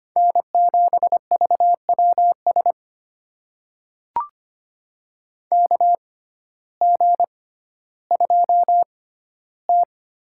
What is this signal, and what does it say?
Morse code, heard as a keyed tone, about 25 wpm, spaced as for 14 wpm, 700 Hz. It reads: N7VWS KG2T
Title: USA Call Signs - 6x Character Spacing - 25wpm